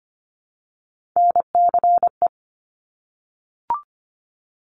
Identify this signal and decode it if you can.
Morse code, heard as a keyed tone, about 25 wpm, 700 Hz. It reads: NCE
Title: Top Three Letter Combinations in Words 25wpm